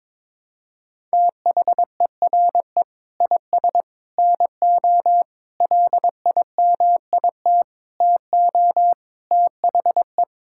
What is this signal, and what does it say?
Morse code, heard as a keyed tone, about 22 wpm, 700 Hz. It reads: THERE IS NO LIMIT TO THE
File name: practice / 22wpm / Sentences with 8-Word Phrases - 22wpm